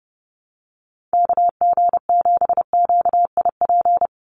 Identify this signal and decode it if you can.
Morse code, heard as a keyed tone, about 30 wpm, 700 Hz. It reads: KG7QIP
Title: USA Call Signs 30wpm